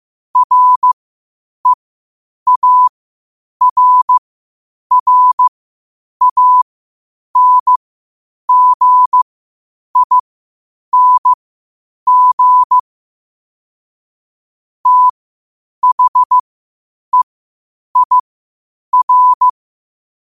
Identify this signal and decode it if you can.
Morse code, heard as a keyed tone, about 15 wpm, spaced as for 8 wpm, 1000 Hz. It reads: REARRANGING THEIR
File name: qotdfiles / 8wpm